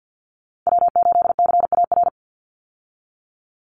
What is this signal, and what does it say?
Morse code, heard as a keyed tone, about 50 wpm, 700 Hz. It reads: W8CAR
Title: Call Signs - 50wpm